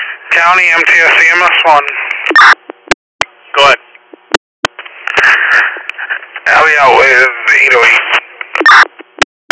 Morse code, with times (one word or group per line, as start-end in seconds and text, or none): none